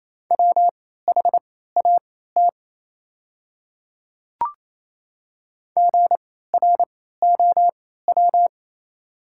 Morse code, heard as a keyed tone, about 28 wpm, spaced as for 16 wpm, 690 Hz. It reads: WHAT GROW
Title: 3x Character Spacing - Top 500 Words - 28wpm